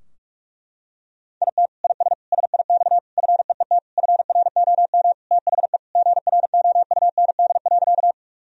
Morse code, heard as a keyed tone, about 42 wpm, 700 Hz. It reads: IT IS SIX FEET FROM THE GROUND.
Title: Sentences from Top 1000 Words - Review Part 2 - 45wpm